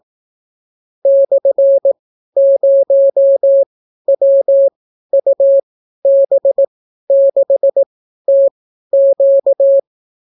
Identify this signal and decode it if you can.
Morse code, heard as a keyed tone, about 18 wpm, spaced as for 12 wpm, 550 Hz. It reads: /0WUB6TQ